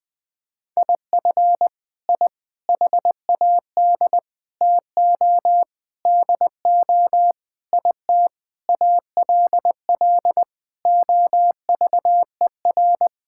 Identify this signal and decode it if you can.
Morse code, heard as a keyed tone, about 20 wpm, 700 Hz. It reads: IF I HAD TO DO IT ALL OVER
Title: Sentences with 9-Word Phrases - 20wpm